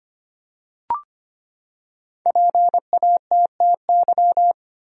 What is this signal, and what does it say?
Morse code, heard as a keyed tone, about 25 wpm, 700 Hz. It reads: PATTY
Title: USA Common Names - Max 5 Letters - 25wpm